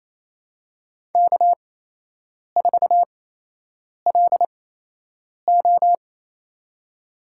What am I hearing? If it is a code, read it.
Morse code, heard as a keyed tone, about 27 wpm, spaced as for 8 wpm, 700 Hz. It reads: K4LO
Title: USA Call Signs - 8x Character Spacing - 28wpm